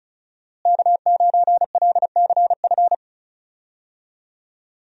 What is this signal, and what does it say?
Morse code, heard as a keyed tone, about 36 wpm, 700 Hz. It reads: K9LCF
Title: USA Call Signs - 5 Characters - Repeated 2x - 35wpm